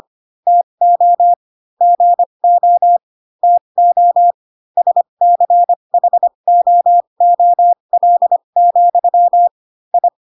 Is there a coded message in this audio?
Yes: Morse code from 0.5 to 10.1 s, heard as a keyed tone, about 25 wpm, spaced as for 22 wpm, 700 Hz. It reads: TO GO TO SCHOOL, I